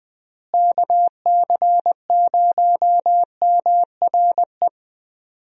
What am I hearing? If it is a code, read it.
Morse code, heard as a keyed tone, about 21 wpm, 700 Hz. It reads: KC0MRE
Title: USA Call Signs - 6 Characters - Repeated 2x - 20wpm